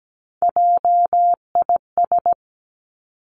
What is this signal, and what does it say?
Morse code, heard as a keyed tone, about 17 wpm, 700 Hz. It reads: JIS
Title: Three Challenging Letter-Number Combinations 17wpm